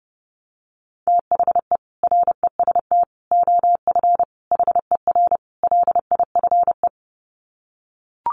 Morse code, heard as a keyed tone, about 28 wpm, 700 Hz. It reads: THE REST OF HER LIFE
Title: Sets of 5 Words Encore 30wpm